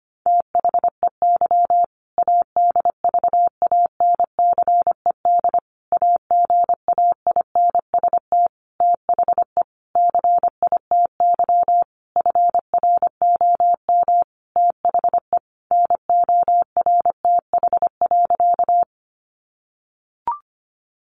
Morse code, heard as a keyed tone, about 25 wpm, 700 Hz. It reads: THEY ADVANCED AGAINST THE CITY FROM THE NORTH.
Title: Sentences from Top 1200 Words 25wpm